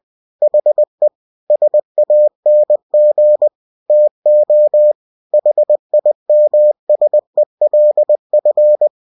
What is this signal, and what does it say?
Morse code, heard as a keyed tone, about 20 wpm, 600 Hz. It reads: HE SANG TO HIMSELF
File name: cw-books / winnie-the-pooh / winnie-20wpm-page-0032